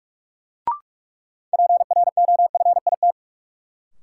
Morse code, heard as a keyed tone, about 42 wpm, 700 Hz. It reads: PROFIT
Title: Sentences from Top 800 Words - New Word Drill - 45wpm